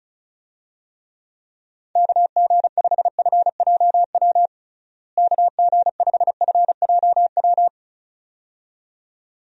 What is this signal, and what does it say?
Morse code, heard as a keyed tone, about 36 wpm, 700 Hz. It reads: KG5FJW KG5FJW
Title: USA Call Signs - 6 Characters - Repeated 2x - 35wpm